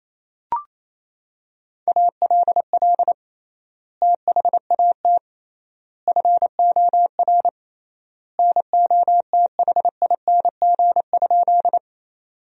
Morse code, heard as a keyed tone, about 28 wpm, 700 Hz. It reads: ALL THAT FOR NOTHING?
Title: 3x Word-Spacing - Sentences from Top 500 Words - 28wpm